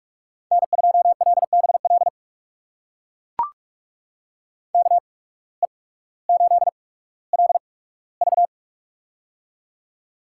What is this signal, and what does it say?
Morse code, heard as a keyed tone, about 44 wpm, spaced as for 19 wpm, 700 Hz. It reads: N2LBL KE8LV
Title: USA Call Signs - 5 Characters - 8x Character Spacing - 45wpm